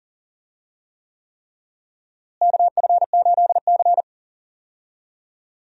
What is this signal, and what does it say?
Morse code, heard as a keyed tone, about 37 wpm, 700 Hz. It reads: KF8C
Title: USA Call Signs - 4 Characters - Repeated 3x - 40wpm